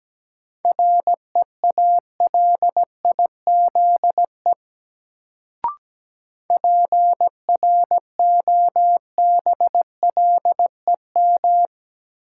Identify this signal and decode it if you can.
Morse code, heard as a keyed tone, about 17 wpm, 700 Hz. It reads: REALIZE PROBLEM
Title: Sentences from Top 1000 Words - New Word Drill - 17wpm